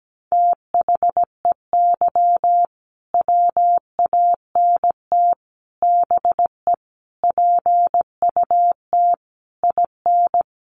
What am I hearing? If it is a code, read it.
Morse code, heard as a keyed tone, about 17 wpm, 700 Hz. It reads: THEY WANT BE PUT IN
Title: Sentences from Top 1000 Words 17wpm